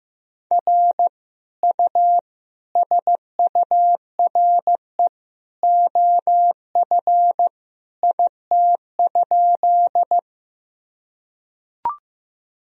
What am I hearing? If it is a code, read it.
Morse code, heard as a keyed tone, about 16 wpm, 700 Hz. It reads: R U SURE OF IT?
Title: Sentences from Top 300 Words and Abbreviations 15wpm